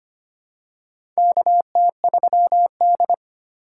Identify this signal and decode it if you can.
Morse code, heard as a keyed tone, about 25 wpm, 700 Hz. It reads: KT3D
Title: USA Call Signs - 4x Character Spacing - 25wpm